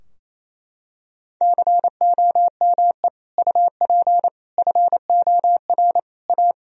Morse code, heard as a keyed tone, about 28 wpm, 700 Hz. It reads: COME UP FOR A
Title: Sentences from Top 1000 Words - Review Part 2 - 28wpm